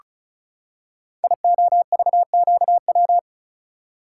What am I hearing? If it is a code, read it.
Morse code, heard as a keyed tone, about 36 wpm, 700 Hz. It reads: IOVQW
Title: Code Groups - 5 Letter:Numbers - 35wpm